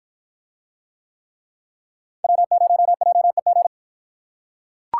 Morse code, heard as a keyed tone, about 48 wpm, 700 Hz. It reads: W0JEG